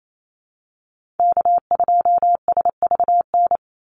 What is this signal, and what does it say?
Morse code, heard as a keyed tone, about 28 wpm, 700 Hz. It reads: K2SVN
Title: USA Call Signs 28wpm